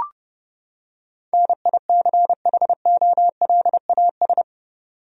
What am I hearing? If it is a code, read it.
Morse code, heard as a keyed tone, about 30 wpm, 700 Hz. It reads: NICHOLAS
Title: USA Common Names - 30wpm